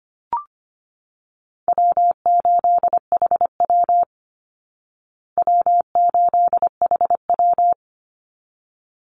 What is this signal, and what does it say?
Morse code, heard as a keyed tone, about 25 wpm, 700 Hz. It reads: W8HW W8HW
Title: Easy Call Signs Repeated 2x - 25wpm